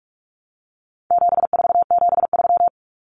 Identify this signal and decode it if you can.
Morse code, heard as a keyed tone, about 44 wpm, 700 Hz. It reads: 7473
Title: Common Cut Numbers - 4-Digit 45wpm